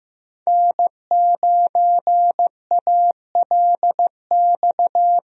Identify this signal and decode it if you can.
Morse code, heard as a keyed tone, about 16 wpm, 700 Hz. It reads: N9ALX
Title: USA Call Signs - 5 Characters - 15wpm